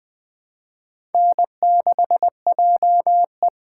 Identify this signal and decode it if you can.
Morse code, heard as a keyed tone, about 20 wpm, 700 Hz. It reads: N6JE
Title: USA Call Signs - 2x Character Spacing - 20wpm